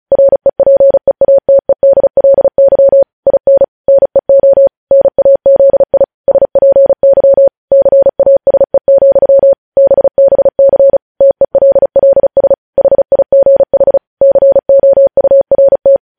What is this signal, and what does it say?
Morse code, heard as a keyed tone, about 36 wpm, 550 Hz. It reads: REPEATEDLY IN NEO NAZI SPY CASE, BBC TELLS HIGH COURT
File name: podcast / News headlines-1x-35x35